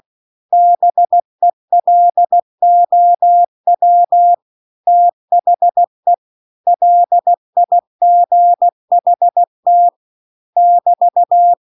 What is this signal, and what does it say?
Morse code, heard as a keyed tone, about 16 wpm, 700 Hz. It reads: BELOW THE LIGHT <BT>
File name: CW-The-Big-Twoheaded-River-2-16WPM